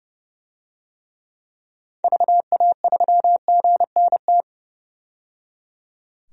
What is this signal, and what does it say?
Morse code, heard as a keyed tone, about 30 wpm, 700 Hz. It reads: VA3GNT
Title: DXCC-100 Call Signs - 6 Characters - 30wpm